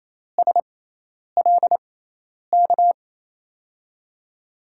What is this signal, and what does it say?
Morse code, heard as a keyed tone, about 28 wpm, spaced as for 10 wpm, 700 Hz. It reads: SLK
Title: USA Call Signs - 6x Character Spacing - 28wpm